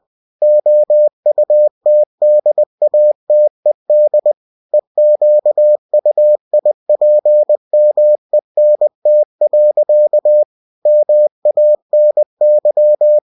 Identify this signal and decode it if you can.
Morse code, heard as a keyed tone, about 20 wpm, 600 Hz. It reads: OUTDATED EQUIPMENT. MANY